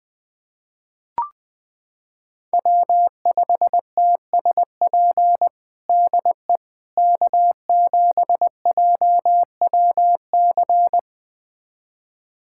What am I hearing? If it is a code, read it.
Morse code, heard as a keyed tone, about 21 wpm, 700 Hz. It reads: W5TSP DE K7JWC
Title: USA Callsign DE Callsign - 5 Characters - 20wpm